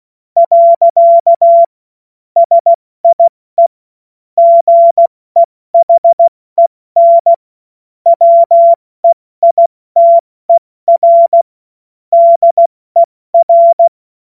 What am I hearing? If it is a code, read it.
Morse code, heard as a keyed tone, about 16 wpm, 680 Hz. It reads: . SIE GEHEN WEITER DER
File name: Morsekurs / Witze / Witz 17. August 2021, 70 F80